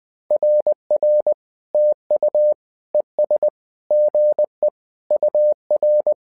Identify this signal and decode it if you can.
Morse code, heard as a keyed tone, about 20 wpm, 600 Hz. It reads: RR TU ES GE UR
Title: SOTA QSO Practice 20wpm